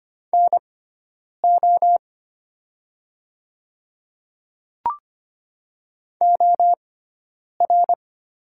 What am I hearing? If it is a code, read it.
Morse code, heard as a keyed tone, about 25 wpm, spaced as for 9 wpm, 700 Hz. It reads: NO OR
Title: 6x Character Spacing - Top 200 Words - 25wpm